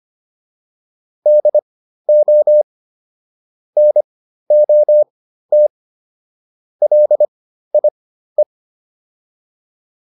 Morse code, heard as a keyed tone, about 25 wpm, spaced as for 13 wpm, 600 Hz. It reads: DO NOT LIE